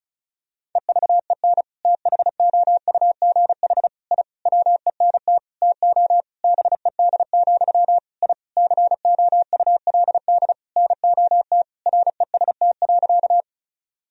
Morse code, heard as a keyed tone, about 34 wpm, 700 Hz. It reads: EVEN THOUGH I WENT TO BED, I COULD NOT REST.
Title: Sentences from Top 1000 Words - Review Part 2 - 35wpm